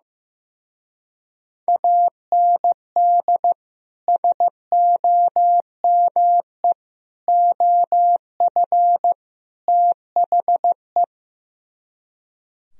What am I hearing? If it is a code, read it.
Morse code, heard as a keyed tone, about 15 wpm, 700 Hz. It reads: AND SOME OF THE
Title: Speed-Racing - Sets of 4 Words 15wpm